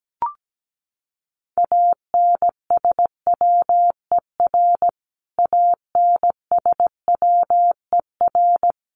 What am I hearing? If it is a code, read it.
Morse code, heard as a keyed tone, about 17 wpm, 700 Hz. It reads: ANSWER ANSWER
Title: Top 500 Words Repeated 17wpm